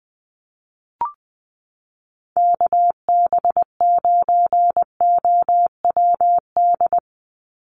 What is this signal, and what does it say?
Morse code, heard as a keyed tone, about 20 wpm, 700 Hz. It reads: KB9OWD
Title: Call Signs - 20wpm